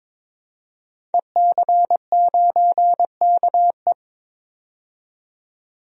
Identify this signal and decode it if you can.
Morse code, heard as a keyed tone, about 22 wpm, 700 Hz. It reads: EC9KE
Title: DXCC-100 Call Signs - 5 Characters - 22wpm